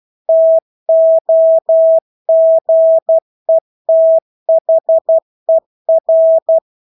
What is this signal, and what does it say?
Morse code, heard as a keyed tone, about 12 wpm, 650 Hz. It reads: TOGETHER
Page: Engelske ord 12 wpm